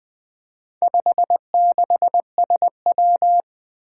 Morse code, heard as a keyed tone, about 20 wpm, 700 Hz. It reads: 56SW
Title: Four Challenging Letter-Number Combinations 20wpm